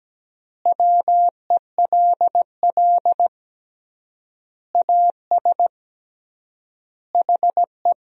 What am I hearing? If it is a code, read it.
Morse code, heard as a keyed tone, about 17 wpm, 700 Hz. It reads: WELL AS HE
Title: 3x Word-Spacing - Sentences from Top 500 Words - 17wpm